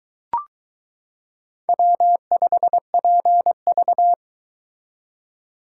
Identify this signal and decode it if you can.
Morse code, heard as a keyed tone, about 22 wpm, 700 Hz. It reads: W5PV